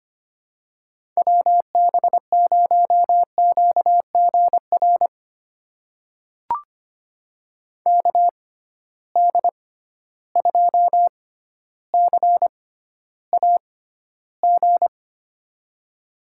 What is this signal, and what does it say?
Morse code, heard as a keyed tone, about 25 wpm, spaced as for 13 wpm, 700 Hz. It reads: WB0QGR KD2CAG
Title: USA Call Signs - 6x Character Spacing - 25wpm